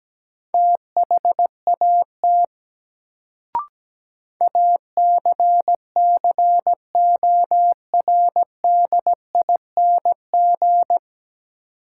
Morse code, heard as a keyed tone, about 17 wpm, 700 Hz. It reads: THAT ACCORDING
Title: Sets of 2 Words - New Word Drill - 17wpm